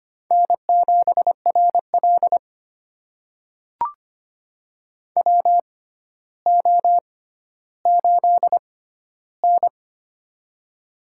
Morse code, heard as a keyed tone, about 25 wpm, spaced as for 13 wpm, 700 Hz. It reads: N7RL WO8N
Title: USA Call Signs - 4 Characters - 6x Character Spacing - 25wpm